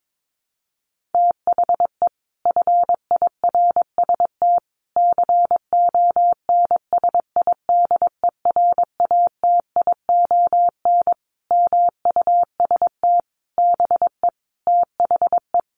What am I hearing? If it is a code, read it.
Morse code, heard as a keyed tone, about 22 wpm, 700 Hz. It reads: THE FIRST CONSIDERATION MUST BE THE
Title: Sentences from Top 1500 Words 22wpm